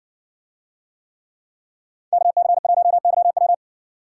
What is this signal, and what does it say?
Morse code, heard as a keyed tone, about 60 wpm, 700 Hz. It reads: KC1YW